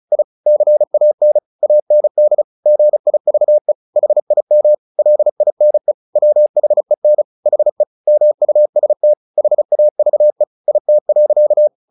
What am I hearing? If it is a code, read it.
Morse code, heard as a keyed tone, about 36 wpm, 600 Hz. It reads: I CAN AND GIVE HIM LINE WHEN HE MUST HAVE IT.
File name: the old man and the sea / the old man and the sea 1 35wpm